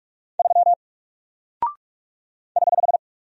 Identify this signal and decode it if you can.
Morse code, heard as a keyed tone, about 44 wpm, 700 Hz. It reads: 3 <HH>
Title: Single Character (Letter:Number:Punctuation:Prosign) Alt 45wpm